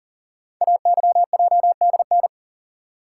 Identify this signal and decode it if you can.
Morse code, heard as a keyed tone, about 40 wpm, 690 Hz. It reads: AYJDN
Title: Code Groups - 5 Letters - 40wpm